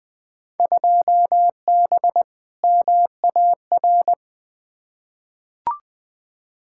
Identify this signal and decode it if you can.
Morse code, heard as a keyed tone, about 20 wpm, 700 Hz. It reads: 2B MAR
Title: Field Day Exchanges Encore - 20wpm